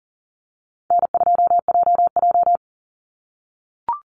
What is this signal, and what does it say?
Morse code, heard as a keyed tone, about 40 wpm, 700 Hz. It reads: N2JJ
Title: Call Signs - 40wpm